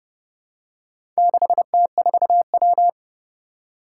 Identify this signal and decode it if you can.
Morse code, heard as a keyed tone, about 30 wpm, 700 Hz. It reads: 6T4W
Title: Four Challenging Letter-Number Combinations 30wpm